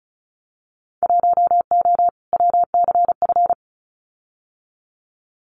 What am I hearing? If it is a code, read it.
Morse code, heard as a keyed tone, about 34 wpm, 700 Hz. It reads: 1O WCF
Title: Winter Field Day Exchanges 35wpm